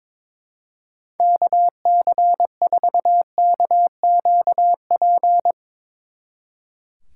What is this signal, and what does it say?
Morse code, heard as a keyed tone, about 24 wpm, 700 Hz. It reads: KC4KQP
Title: Speed Racing - USA Call Signs 22wpm